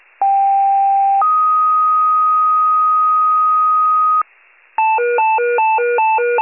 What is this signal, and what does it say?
Two-tone page, heard as a one-second tone, then a three-second tone, about 790 then 1250 Hz. It is heard from 0.2 to 4.2 s.